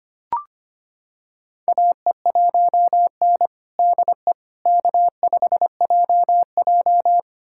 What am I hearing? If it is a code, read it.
Morse code, heard as a keyed tone, about 25 wpm, 700 Hz. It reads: AE1N DE K5JJ
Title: USA Callsign DE Callsign - 4 Characters - 25wpm